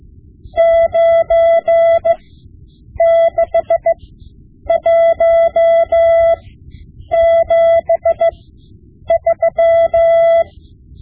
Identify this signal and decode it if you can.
Morse code, heard as a keyed tone, about 15 wpm, spaced as for 8 wpm, 670 Hz. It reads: 96173
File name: Morse Lessons / Morse Lessons / Tape-5B